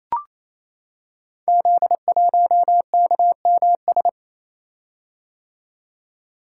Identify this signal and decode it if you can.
Morse code, heard as a keyed tone, about 29 wpm, 700 Hz. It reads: Z1KMS
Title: Code Groups - 5 Letter:Numbers - 28wpm